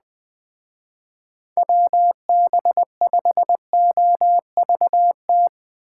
Morse code, heard as a keyed tone, about 21 wpm, 700 Hz. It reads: WB5OVT
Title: Speed Racing - USA Call Signs 17wpm